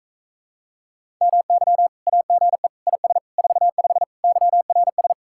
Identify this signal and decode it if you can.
Morse code, heard as a keyed tone, about 45 wpm, 700 Hz. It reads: MY AGE IS 45 YRS